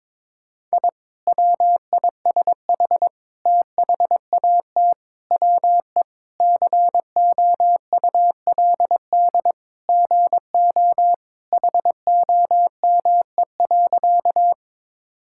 Morse code, heard as a keyed tone, about 22 wpm, 700 Hz. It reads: I WISH THAT WE COULD GO HOME.
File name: practice / 22wpm / Sentences with 2-Word Phrases - 22wpm